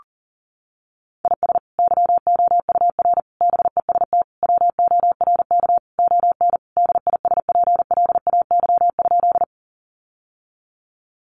Morse code, heard as a keyed tone, about 40 wpm, 700 Hz. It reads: IS YOUR BEST WORK ON DISPLAY?
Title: Sentences from Top 1200 Words 40wpm